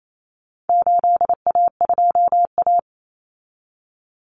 Morse code, heard as a keyed tone, about 28 wpm, 700 Hz. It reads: 8A2A
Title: Common Cut Numbers - 4-Digit 28wpm